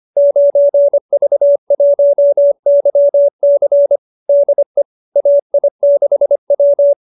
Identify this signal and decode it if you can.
Morse code, heard as a keyed tone, about 25 wpm, 570 Hz. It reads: 9V1YC DE AI6W
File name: qso207b 25